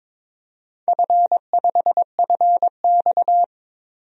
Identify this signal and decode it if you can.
Morse code, heard as a keyed tone, about 22 wpm, 700 Hz. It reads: F5FX
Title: DXCC-100 Call Signs - 4 Characters - 22wpm